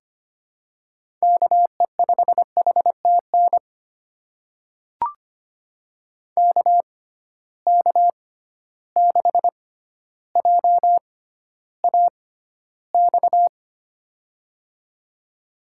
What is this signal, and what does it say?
Morse code, heard as a keyed tone, about 25 wpm, spaced as for 13 wpm, 700 Hz. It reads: KE5HTN KK6JAX
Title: USA Call Signs - 6x Character Spacing - 25wpm